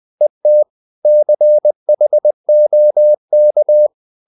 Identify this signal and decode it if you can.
Morse code, heard as a keyed tone, about 20 wpm, 600 Hz. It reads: ET CHOK
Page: Ferie 20 wpm